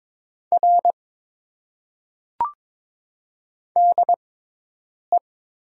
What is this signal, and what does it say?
Morse code, heard as a keyed tone, about 22 wpm, spaced as for 8 wpm, 700 Hz. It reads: R DE
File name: Top 90 QSO Elements - 6x Character Spacing - 22wpm